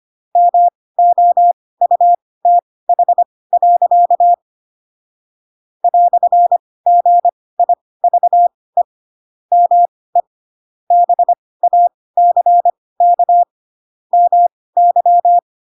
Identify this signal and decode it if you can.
Morse code, heard as a keyed tone, about 25 wpm, spaced as for 18 wpm, 700 Hz. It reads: MOUTH. "GIVE ME BACK MY
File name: Pinocchio / Pinocchio ch2 12 25 ik1hge